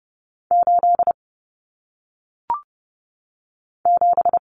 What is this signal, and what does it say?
Morse code, heard as a keyed tone, about 30 wpm, 700 Hz. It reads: 8 7
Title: Common Cut Numbers - 1-Digit 30wpm